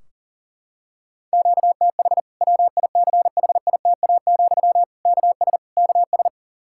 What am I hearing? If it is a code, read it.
Morse code, heard as a keyed tone, about 40 wpm, 700 Hz. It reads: QTH WICHITA, KS KS
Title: Practice QSO - Exchange 1 - Beginning - 40wpm